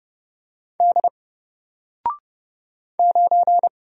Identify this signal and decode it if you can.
Morse code, heard as a keyed tone, about 30 wpm, 700 Hz. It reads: D 9
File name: Single-Letter-Number-30wpm